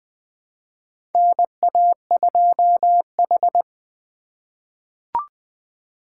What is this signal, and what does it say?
Morse code, heard as a keyed tone, about 21 wpm, 700 Hz. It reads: NA2H
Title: USA Call Signs - 4 Characters - 3x Character Spacing - 20wpm